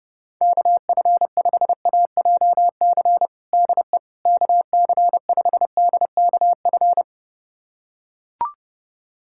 Morse code, heard as a keyed tone, about 28 wpm, 700 Hz. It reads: KF5AJC DE KC5DKF
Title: USA Callsign DE Callsign - 6 Characters - 30wpm